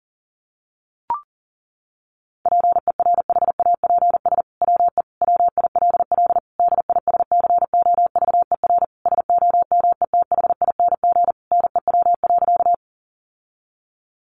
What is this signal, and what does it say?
Morse code, heard as a keyed tone, about 40 wpm, 700 Hz. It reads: PERHAPS WE WILL DISCOVER SOMETHING NEW.
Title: Sentences from Top 1100 Words 40wpm